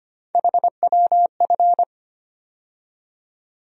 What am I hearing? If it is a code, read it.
Morse code, heard as a keyed tone, about 25 wpm, 700 Hz. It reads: HWF
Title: USA Call Sign Postfixes - 3 Letters 25wpm